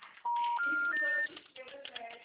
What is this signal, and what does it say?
Special information tone, with speech.
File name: out- -2030